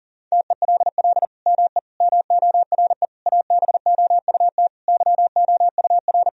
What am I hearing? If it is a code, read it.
Morse code, heard as a keyed tone, about 40 wpm, 700 Hz. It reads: TELL ME MORE ABOUT YOUR
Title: Sentences from Top 1000 Words - Review Part 4 - 40wpm